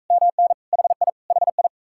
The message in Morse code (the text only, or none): MN HI HI